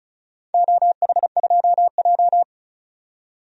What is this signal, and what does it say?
Morse code, heard as a keyed tone, about 36 wpm, 700 Hz. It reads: OH2J
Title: DXCC-100 Call Signs - 4 Characters - 35wpm